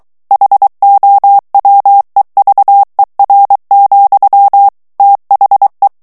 Morse code, heard as a keyed tone, about 23 wpm, 800 Hz. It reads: HOWEVER, THE
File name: morse mp3 / ANGLAIS texte clair 1 vitesse 22 mots minute